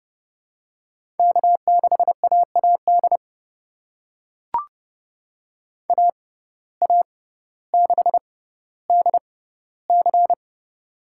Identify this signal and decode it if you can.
Morse code, heard as a keyed tone, about 31 wpm, spaced as for 16 wpm, 700 Hz. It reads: K6AAD AA6DC
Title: USA Call Signs - 6x Character Spacing - 30wpm